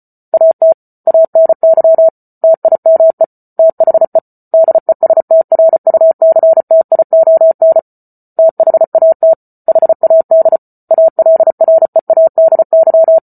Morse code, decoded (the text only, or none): AT ANY TIME THE DESTRUCTION THAT HAD ALREADY